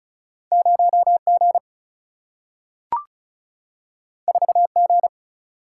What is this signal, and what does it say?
Morse code, heard as a keyed tone, about 36 wpm, 700 Hz. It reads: 0G 4G
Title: Single-Letter Single-Number 35wpm